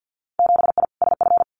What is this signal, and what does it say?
Morse code, heard as a keyed tone, about 50 wpm, 700 Hz. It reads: 7I SF
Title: Winter Field Day Exchanges Encore 50wpm